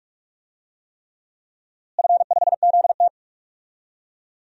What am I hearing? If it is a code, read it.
Morse code, heard as a keyed tone, about 46 wpm, 700 Hz. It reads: F5ZT
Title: DXCC-100 Call Signs - 4 Characters - Repeated 3x - 45wpm